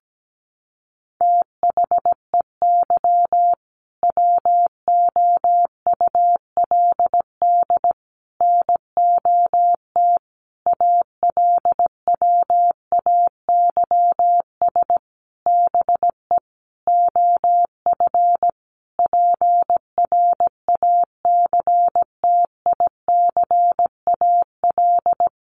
Morse code, heard as a keyed tone, about 17 wpm, 700 Hz. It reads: THEY WOULD NOT ALWAYS BE OF PRACTICAL
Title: Sentences from Top 1400 Words 17wpm